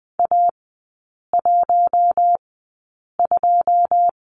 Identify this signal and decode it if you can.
Morse code, heard as a keyed tone, about 20 wpm, spaced as for 8 wpm, 690 Hz. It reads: A12
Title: 159 Lesson 26e - Letter g - Three Characters 20wpm